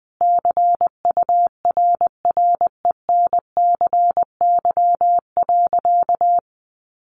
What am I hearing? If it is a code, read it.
Morse code, heard as a keyed tone, about 20 wpm, 700 Hz. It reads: CURRENCY.
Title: Sentences from Top 1500 Words 20wpm